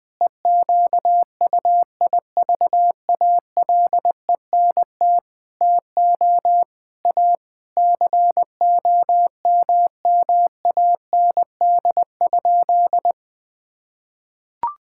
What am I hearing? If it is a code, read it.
Morse code, heard as a keyed tone, about 20 wpm, 700 Hz. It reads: EQUIVALENT TO A COMMAND?
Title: Sentences from Top 1600 Words 20wpm